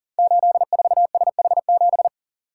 Morse code, heard as a keyed tone, about 38 wpm, 690 Hz. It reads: 84SH7
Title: Code Groups - 5 Letter:Numbers - 40wpm